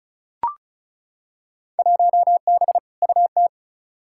Morse code, heard as a keyed tone, about 34 wpm, 700 Hz. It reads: 1B UT
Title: Field Day Exchanges - 35wpm